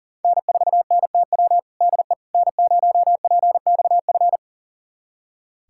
Morse code, heard as a keyed tone, about 40 wpm, 700 Hz. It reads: N4NTW DE N0PXF
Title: USA Callsign DE Callsign - 5 Characters - 40wpm